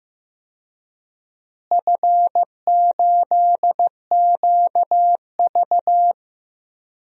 Morse code, heard as a keyed tone, about 15 wpm, 700 Hz. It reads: F8QV
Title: DXCC-100 Call Signs - 4 Characters - Repeated 3x - 15wpm